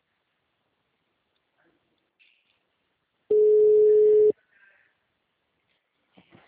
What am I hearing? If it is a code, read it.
Ringback tone (Europe, 425 Hz).